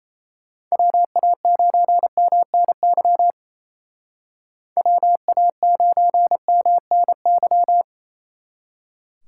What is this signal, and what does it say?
Morse code, heard as a keyed tone, about 34 wpm, 700 Hz. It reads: WA9MNY WA9MNY
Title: Speed Racing - USA 2x3 Call Signs 28wpm